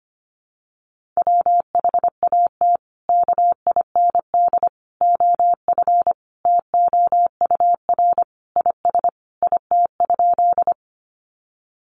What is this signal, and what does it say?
Morse code, heard as a keyed tone, about 25 wpm, 700 Hz. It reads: WHAT KIND OF TOUR IS IT?
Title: Sentences from Top 1400 Words 25wpm